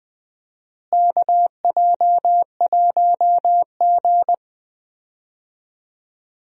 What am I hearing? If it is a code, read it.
Morse code, heard as a keyed tone, about 20 wpm, 700 Hz. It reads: KJ1G
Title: USA Call Signs - 4 Characters - Repeated 3x - 20wpm